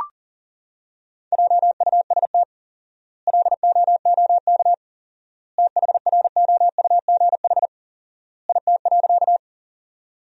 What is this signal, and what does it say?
Morse code, heard as a keyed tone, about 40 wpm, 700 Hz. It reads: JUST LOOK THROUGH IT.
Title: 4x Word-Spacing - Sentences from Top 200 Words - 40wpm